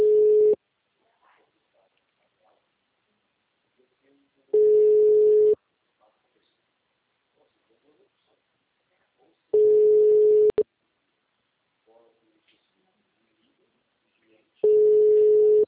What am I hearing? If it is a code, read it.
Ringback tone (Europe, 425 Hz), with speech.